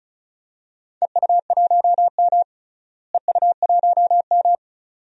Morse code, heard as a keyed tone, about 36 wpm, 700 Hz. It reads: EU1M EU1M
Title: DXCC-100 Call Signs - 4 Characters - Repeated 2x - 35wpm